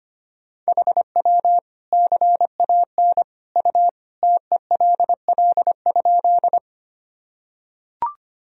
Morse code, heard as a keyed tone, about 25 wpm, 700 Hz. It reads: HW CAN U TELL?
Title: Sentences from Top 200 Words and Abbreviations 25wpm